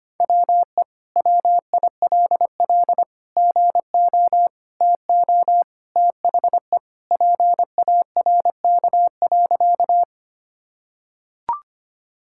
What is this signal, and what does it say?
Morse code, heard as a keyed tone, about 25 wpm, 700 Hz. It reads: WE WILL GO TO THE PARK.
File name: practice / 25wpm / Sentences with 2-Word Phrases - 25wpm